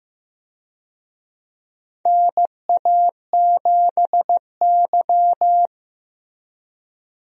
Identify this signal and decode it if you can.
Morse code, heard as a keyed tone, about 15 wpm, 700 Hz. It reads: NA7Y
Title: USA Call Signs - 4 Characters - Repeated 3x - 15wpm